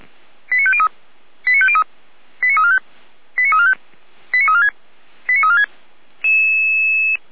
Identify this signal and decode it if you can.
Five-tone selective call (ZVEI), heard as quick runs of five tones, tones of about 70 ms. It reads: 89503 89503 89345 89345 89346 89346